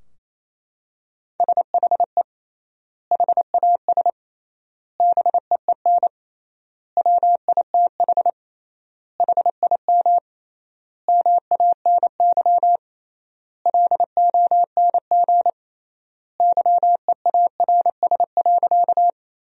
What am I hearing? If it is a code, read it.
Morse code, heard as a keyed tone, about 28 wpm, 700 Hz. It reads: SHE HAS BEEN WITH HIM MANY LONG YEARS.
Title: 3x Word-Spacing - Sentences from Top 200 Words - 28wpm